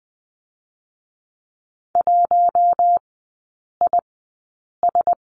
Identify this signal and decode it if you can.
Morse code, heard as a keyed tone, about 20 wpm, spaced as for 8 wpm, 700 Hz. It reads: 1IS
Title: 133 Lesson 22e - Question Mark - Three Characters 20wpm